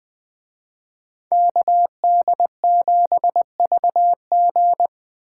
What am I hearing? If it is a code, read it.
Morse code, heard as a keyed tone, about 20 wpm, 700 Hz. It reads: KD7VG
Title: Speed Racing - USA 2x2 Call Signs 17wpm